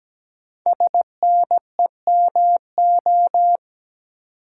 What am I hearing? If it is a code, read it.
Morse code, heard as a keyed tone, about 17 wpm, 700 Hz. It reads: SNEMO
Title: Code Groups - 5 Letters - 17wpm